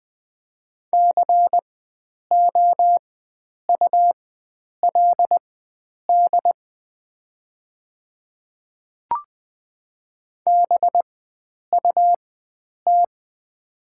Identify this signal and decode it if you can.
Morse code, heard as a keyed tone, about 20 wpm, spaced as for 9 wpm, 700 Hz. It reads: COULD BUT
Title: 4x Character Spacing - Top 100 Words - 20wpm